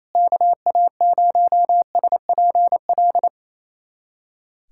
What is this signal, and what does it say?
Morse code, heard as a keyed tone, about 28 wpm, 700 Hz. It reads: KA0SPL
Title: USA Callsign DE Callsign - 6 Characters - 28wpm